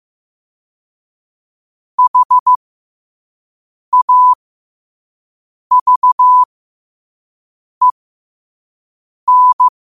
Morse code, heard as a keyed tone, about 15 wpm, spaced as for 5 wpm, 1000 Hz. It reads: HAVEN
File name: qotdfiles / 5wpm